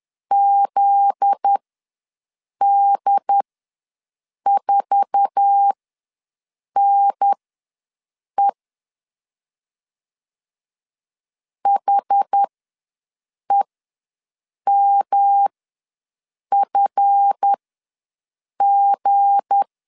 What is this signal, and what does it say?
Morse code, heard as a keyed tone, about 10 wpm, spaced as for 6 wpm, 800 Hz. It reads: ZD4NE HEMFG
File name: LEZIONI / 8 YJ45 / 8 YJ4 ws 10wpm 4020